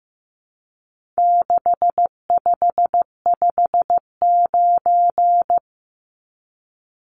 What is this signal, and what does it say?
Morse code, heard as a keyed tone, about 15 wpm, 700 Hz. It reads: 6559
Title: Numbers - 4-Digit 15wpm